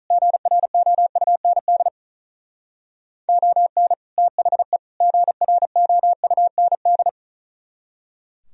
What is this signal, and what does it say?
Morse code, heard as a keyed tone, about 43 wpm, 700 Hz. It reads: GROUND ON THE GROUND
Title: Speed-Racing - Sets of 3 Words 35wpm